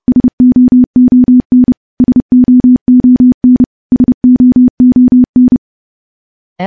Morse code, heard as a keyed tone, about 28 wpm, 250 Hz. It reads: SOON SOON SOON
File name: morse / morse-training-30WPM